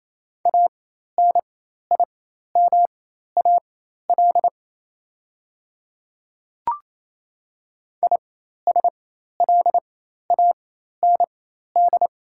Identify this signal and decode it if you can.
Morse code, heard as a keyed tone, about 28 wpm, spaced as for 13 wpm, 690 Hz. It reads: ANIMAL ISLAND
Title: 4x Character Spacing - Top 500 Words - 28wpm